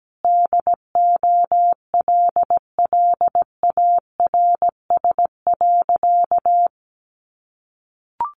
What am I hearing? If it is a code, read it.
Morse code, heard as a keyed tone, about 17 wpm, 700 Hz. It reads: DOLLARS.
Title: Additional Easy Phrases 17wpm